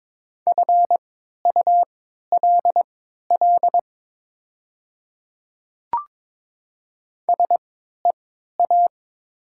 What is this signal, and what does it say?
Morse code, heard as a keyed tone, about 22 wpm, spaced as for 13 wpm, 700 Hz. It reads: FULL SEA
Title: 3x Character Spacing - Top 500 Words - 22wpm